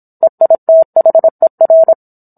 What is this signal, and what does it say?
Morse code, heard as a keyed tone, about 27 wpm, 650 Hz. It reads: EITHER